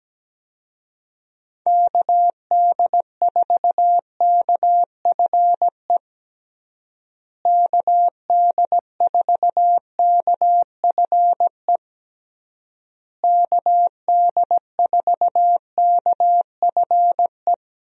Morse code, heard as a keyed tone, about 17 wpm, 700 Hz. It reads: KD4KFE KD4KFE KD4KFE
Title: USA Call Signs - 6 Characters - Repeated 3x - 17wpm